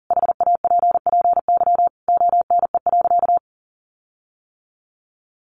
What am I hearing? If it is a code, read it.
Morse code, heard as a keyed tone, about 37 wpm, 700 Hz. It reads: HAPPY ONE.
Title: Sentences from Top 1400 Words 40wpm